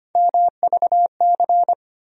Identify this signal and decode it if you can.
Morse code, heard as a keyed tone, about 25 wpm, 700 Hz. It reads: MVC
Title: USA Call Sign Postfixes - 3 Letters 25wpm